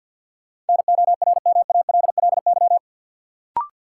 Morse code, heard as a keyed tone, about 50 wpm, 700 Hz. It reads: NORMALLY
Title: Sentences from Top 1000 Words - Review Part 6 - New Word Drill - 50wpm